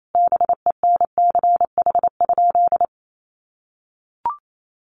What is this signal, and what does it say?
Morse code, heard as a keyed tone, about 28 wpm, 700 Hz. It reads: BENCH?
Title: Sentences from Top 1400 Words 28wpm